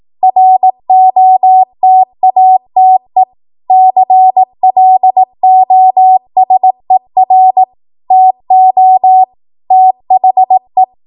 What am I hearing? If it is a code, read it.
Morse code, heard as a keyed tone, about 18 wpm, 750 Hz. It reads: ROTATE CLOSER TO THE
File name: Morse / 18WPM